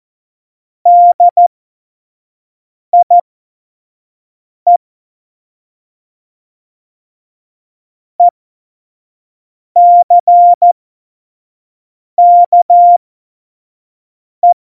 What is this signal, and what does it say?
Morse code, heard as a keyed tone, about 14 wpm, spaced as for 5 wpm, 700 Hz. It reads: DIE ECKE
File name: Morsekurs / Witze / Witz 18. Februar 2020, 25 F70